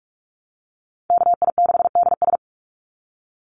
Morse code, heard as a keyed tone, about 44 wpm, 700 Hz. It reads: KI6DS
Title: NA SOTA Chaser Callsigns 45wpm